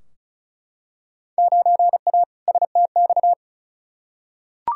Morse code, heard as a keyed tone, about 34 wpm, 700 Hz. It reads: 9A STX
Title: Field Day Exchanges - 35wpm